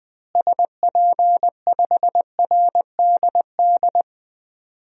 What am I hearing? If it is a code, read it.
Morse code, heard as a keyed tone, about 21 wpm, 700 Hz. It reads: SP5RDD
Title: DXCC-100 Call Signs - 6 Characters - Repeated 2x - 20wpm